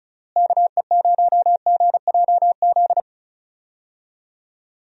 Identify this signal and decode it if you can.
Morse code, heard as a keyed tone, about 34 wpm, 700 Hz. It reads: KE0GJZ
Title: USA Call Signs - 6 Characters - Repeated 2x - 35wpm